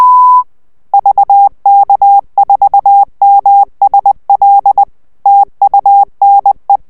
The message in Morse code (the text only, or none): VK4MSL TUNE